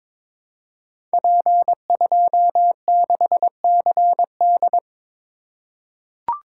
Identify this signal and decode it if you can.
Morse code, heard as a keyed tone, about 22 wpm, 700 Hz. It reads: P26CD
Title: Code Groups - 5 Letter:Numbers - 22wpm